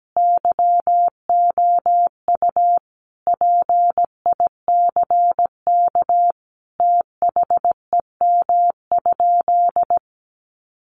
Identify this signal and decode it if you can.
Morse code, heard as a keyed tone, about 17 wpm, 700 Hz. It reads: YOU PICK THEM?
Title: Sentences from Top 1000 Words 17wpm